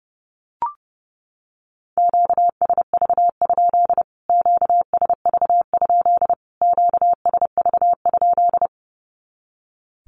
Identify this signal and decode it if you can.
Morse code, heard as a keyed tone, about 30 wpm, 700 Hz. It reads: QSV? QSV? QSV?
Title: Q-Codes Repeated 3x 30wpm